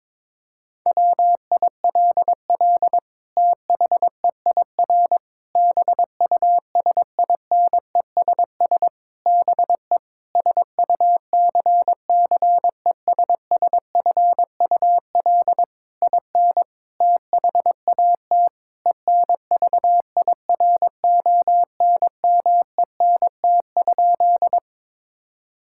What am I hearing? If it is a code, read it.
Morse code, heard as a keyed tone, about 22 wpm, 700 Hz. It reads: WILL THEIR BUSINESS BE SUCCESSFUL IN THAT ENVIRONMENT?
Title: Sentences from Top 1000 Words - Review Part 5 - 22wpm